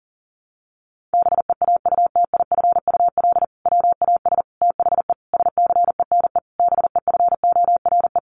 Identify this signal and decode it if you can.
Morse code, heard as a keyed tone, about 40 wpm, 700 Hz. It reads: BEAUTIFUL WAS THE SCENE BEFORE
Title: Sentences from Top 1100 Words 40wpm